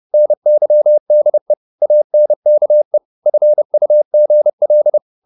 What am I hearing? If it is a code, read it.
Morse code, heard as a keyed tone, about 30 wpm, 600 Hz. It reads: NYDE ANKE FUGL
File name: Korte ord DK 30wpm